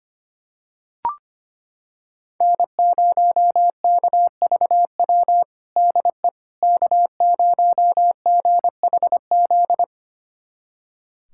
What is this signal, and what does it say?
Morse code, heard as a keyed tone, about 25 wpm, 700 Hz. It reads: N0KVW DE K0GHZ
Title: USA Callsign DE Callsign - 5 Characters - 25wpm